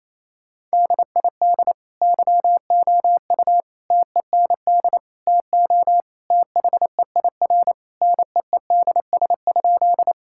Morse code, heard as a keyed tone, about 28 wpm, 700 Hz. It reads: DID YOU TEND TO THEIR NEEDS?
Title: Sentences from Top 1000 Words - Review Part 5 - 28wpm